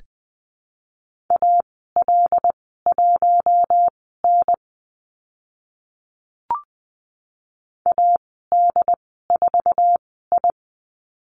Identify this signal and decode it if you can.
Morse code, heard as a keyed tone, about 20 wpm, spaced as for 15 wpm, 700 Hz. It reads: AL1N AD4I
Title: 082 Lesson 14g - Number 2 - Call Signs 20wpm